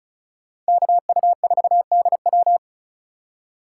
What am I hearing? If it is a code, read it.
Morse code, heard as a keyed tone, about 45 wpm, 700 Hz. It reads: KU4DW